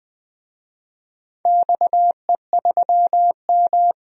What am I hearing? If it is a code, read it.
Morse code, heard as a keyed tone, about 20 wpm, 700 Hz. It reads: XE3M
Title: DXCC-100 Call Signs - 4 Characters - 20wpm